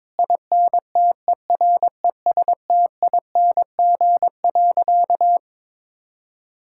Morse code, heard as a keyed tone, about 22 wpm, 700 Hz. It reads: INTERESTING.
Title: Sentences from Top 1000 Words - Review Part 5 - 22wpm